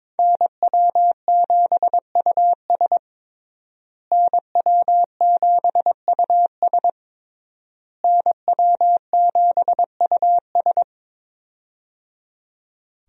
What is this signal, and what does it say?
Morse code, heard as a keyed tone, about 22 wpm, 700 Hz. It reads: NW7US NW7US NW7US
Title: USA Call Signs - 5 Characters - Repeated 3x - 22wpm